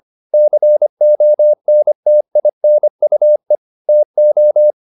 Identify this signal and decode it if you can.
Morse code, heard as a keyed tone, about 25 wpm, 600 Hz. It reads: CONTINUE TO
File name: the old man and the sea part 1 25wpm